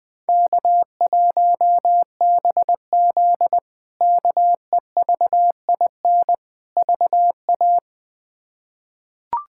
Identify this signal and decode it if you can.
Morse code, heard as a keyed tone, about 20 wpm, 700 Hz. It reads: K1BZ KEVIN VA
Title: K1USN SST QSO Practice - 20wpm